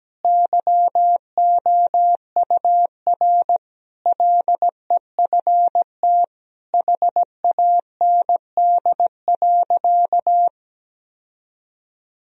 Morse code, heard as a keyed tone, about 17 wpm, 700 Hz. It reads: YOUR LEFT HAND.
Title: Sentences from Top 1000 Words - Review Part 2 - 17wpm